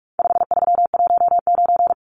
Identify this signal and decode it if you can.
Morse code, heard as a keyed tone, about 44 wpm, 700 Hz. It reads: 5319
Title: Numbers - 4-Digit 45wpm